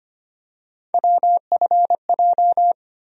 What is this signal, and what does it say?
Morse code, heard as a keyed tone, about 25 wpm, 700 Hz. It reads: WFJ
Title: USA Call Sign Postfixes - 3 Letters 25wpm